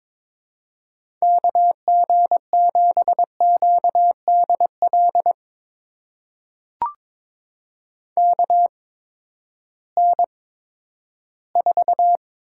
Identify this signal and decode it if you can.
Morse code, heard as a keyed tone, about 22 wpm, spaced as for 12 wpm, 700 Hz. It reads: KG7QDL KN4
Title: USA Call Signs - 6 Characters - 8x Character Spacing - 22wpm